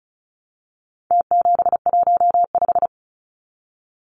Morse code, heard as a keyed tone, about 34 wpm, 700 Hz. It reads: T715
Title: Numbers - Contest Style - 4-Digit 35wpm